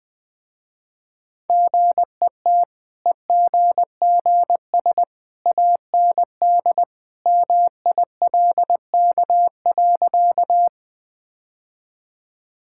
Morse code, heard as a keyed tone, about 21 wpm, 700 Hz. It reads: GET EGGS AND MILK.
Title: Easy Sentences with Hints - 20wpm